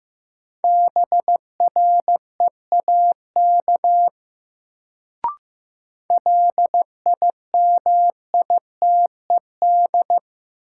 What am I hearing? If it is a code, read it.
Morse code, heard as a keyed tone, about 15 wpm, 700 Hz. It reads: BREAK LIMITED
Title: Sentences from Top 900 Words - New Word Drill - 15wpm